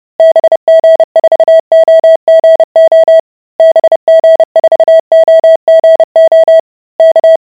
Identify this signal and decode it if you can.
Morse code, heard as a keyed tone, about 30 wpm, 650 Hz. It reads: BG4OGO BG4OGO K